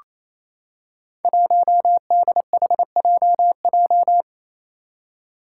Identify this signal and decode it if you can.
Morse code, heard as a keyed tone, about 29 wpm, 700 Hz. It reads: 1DHJJ
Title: Five Challenging Letter-Number Combinations 28wpm